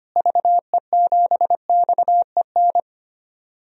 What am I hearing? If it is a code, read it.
Morse code, heard as a keyed tone, about 25 wpm, 700 Hz. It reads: VE7XEN
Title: DXCC-100 Call Signs - 6 Characters - Repeated 2x - 25wpm